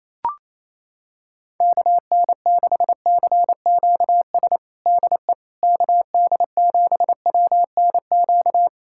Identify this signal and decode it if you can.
Morse code, heard as a keyed tone, about 27 wpm, 700 Hz. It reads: KN6CQS DE KD7WNQ
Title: USA Callsign DE Callsign - 6 Characters - 28wpm